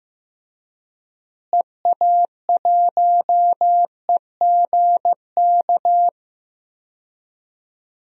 Morse code, heard as a keyed tone, about 15 wpm, 700 Hz. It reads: EA1EGK
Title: DXCC-100 Call Signs - 6 Characters - Repeated 2x - 15wpm